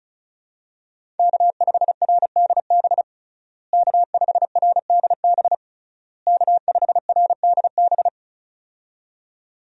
Morse code, heard as a keyed tone, about 34 wpm, 700 Hz. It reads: K5RDB K5RDB K5RDB
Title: USA Call Signs - 5 Characters - Repeated 3x - 35wpm